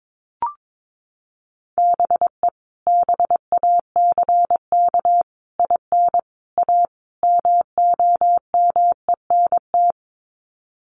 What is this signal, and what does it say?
Morse code, heard as a keyed tone, about 22 wpm, 700 Hz. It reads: BE BACK IN A MOMENT
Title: Sets of 5 Words 22wpm